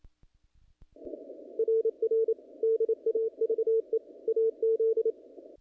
Morse code, heard as a keyed tone, about 27 wpm, 450 Hz. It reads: RR DAVE AZ
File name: contests / naqp-cw-k5tr-op-k5pi- -1900